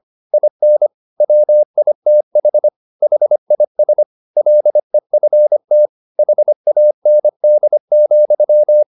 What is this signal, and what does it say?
Morse code, heard as a keyed tone, about 25 wpm, 600 Hz. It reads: IN WITH HIS LEFT HAND,